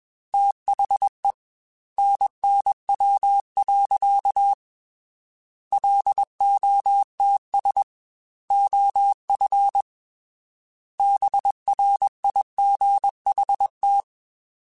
Morse code, heard as a keyed tone, about 21 wpm, 800 Hz. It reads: THE NNW. LOTS OF BRIGHT
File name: Weather Sets / Weather Sets-mp3s / Weather 5 12 @20